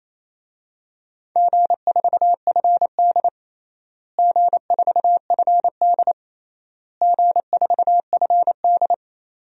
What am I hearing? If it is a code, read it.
Morse code, heard as a keyed tone, about 28 wpm, 700 Hz. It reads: G4FD G4FD G4FD
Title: DXCC-100 Call Signs - 4 Characters - Repeated 3x - 28wpm